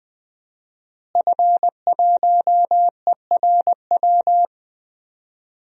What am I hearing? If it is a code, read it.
Morse code, heard as a keyed tone, about 20 wpm, 700 Hz. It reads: F1ERW
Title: DXCC-100 Call Signs - 5 Characters - Repeated 2x - 20wpm